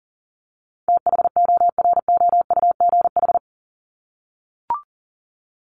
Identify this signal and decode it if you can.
Morse code, heard as a keyed tone, about 40 wpm, 700 Hz. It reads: THOROUGH
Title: LY Suffix - Top 100 Words with Base Words 40wpm